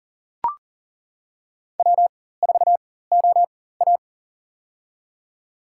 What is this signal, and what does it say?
Morse code, heard as a keyed tone, about 42 wpm, spaced as for 19 wpm, 700 Hz. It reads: W4OA
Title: USA Call Signs - 4 Characters - 4x Character Spacing - 40wpm